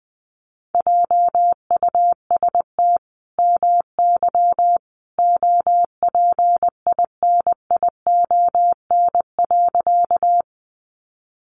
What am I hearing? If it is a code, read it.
Morse code, heard as a keyed tone, about 19 wpm, 700 Hz. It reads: JUST MY OPINION.
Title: Sentences from Top 1200 Words 20wpm